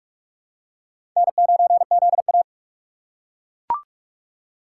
Morse code, heard as a keyed tone, about 42 wpm, 700 Hz. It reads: N9ZA
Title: USA Call Signs - 4 Characters - 4x Character Spacing - 45wpm